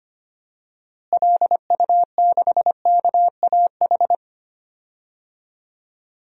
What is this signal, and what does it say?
Morse code, heard as a keyed tone, about 25 wpm, 700 Hz. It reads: LU6KAH
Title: DXCC-100 Call Signs - 6 Characters - Repeated 2x - 25wpm